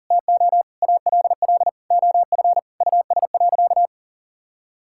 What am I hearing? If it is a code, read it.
Morse code, heard as a keyed tone, about 40 wpm, 700 Hz. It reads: TO ALL OF US.
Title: Sentences from Top 1000 Words - Review Part 6 - 40wpm